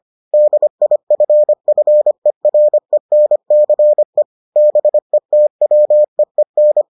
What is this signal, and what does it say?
Morse code, heard as a keyed tone, about 25 wpm, 600 Hz. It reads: DIFFERENCE BETWEEN